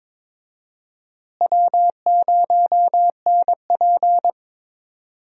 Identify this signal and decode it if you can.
Morse code, heard as a keyed tone, about 22 wpm, 700 Hz. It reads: W0NP
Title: USA Call Signs - 4 Characters - Repeated 2x - 22wpm